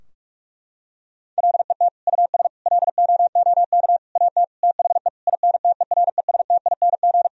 Morse code, heard as a keyed tone, about 41 wpm, 700 Hz. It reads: LET US LOOK AT THE INTERESTING
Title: Sentences from Top 1000 Words - Review Part 2 - 45wpm